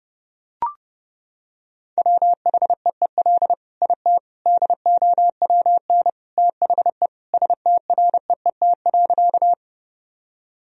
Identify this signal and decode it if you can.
Morse code, heard as a keyed tone, about 30 wpm, 700 Hz. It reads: WHEEL IT DOWN THE STREET.
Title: Sentences from Top 1000 Words - Review Part 3 - 30wpm